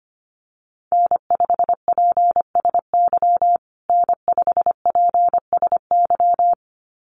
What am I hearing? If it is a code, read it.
Morse code, heard as a keyed tone, about 25 wpm, 700 Hz. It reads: N5PSY N5PSY
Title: USA Call Signs Repeated 2x 25wpm